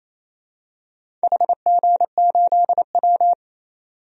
Morse code, heard as a keyed tone, about 27 wpm, 690 Hz. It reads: HG8W
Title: DXCC-100 Call Signs - 4 Characters - 28wpm